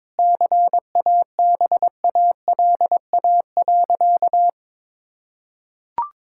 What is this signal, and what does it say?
Morse code, heard as a keyed tone, about 22 wpm, 700 Hz. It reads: CABALA.
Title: Pangram Challenge v2 - 22wpm